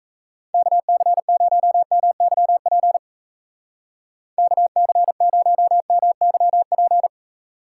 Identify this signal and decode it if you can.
Morse code, heard as a keyed tone, about 36 wpm, 700 Hz. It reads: KC0MYP KC0MYP